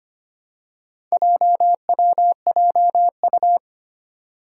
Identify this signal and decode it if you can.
Morse code, heard as a keyed tone, about 25 wpm, 700 Hz. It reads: JWJU
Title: Four Challenging Letter-Number Combinations 25wpm